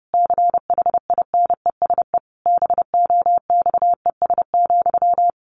Morse code, heard as a keyed tone, about 30 wpm, 700 Hz. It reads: CHINESE BOXES,
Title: A Sound Of Thunder 30wpm